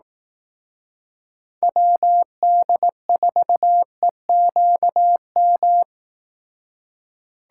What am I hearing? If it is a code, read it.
Morse code, heard as a keyed tone, about 18 wpm, 700 Hz. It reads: WD4EQM
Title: Speed Racing - USA 2x3 Call Signs 15wpm